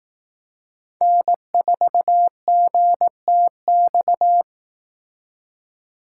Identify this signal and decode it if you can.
Morse code, heard as a keyed tone, about 18 wpm, 700 Hz. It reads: N4GTX
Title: Speed Racing - USA 1x3 Call Signs 15wpm